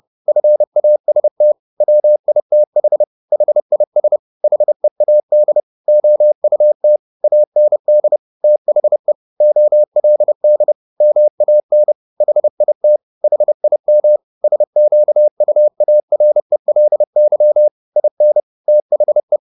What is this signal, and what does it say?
Morse code, heard as a keyed tone, about 30 wpm, 600 Hz. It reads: FAST WITH HIS HEAD OUT AND THE OLD MAN HIT HIM SQUARELY IN THE